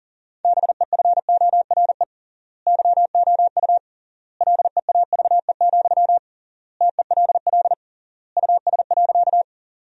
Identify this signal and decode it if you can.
Morse code, heard as a keyed tone, about 40 wpm, 700 Hz. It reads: BEFORE YOU LEAVE, TELL US.
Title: 3x Word-Spacing - Sentences from Top 400 Words - 40wpm